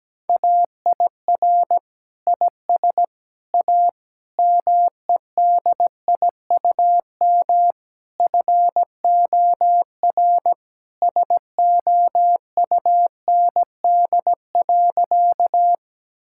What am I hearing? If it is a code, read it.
Morse code, heard as a keyed tone, about 17 wpm, 700 Hz. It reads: AIR IS A MEDIUM FOR SOUND.
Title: Sentences from Top 1000 Words - Review Part 6 - 17wpm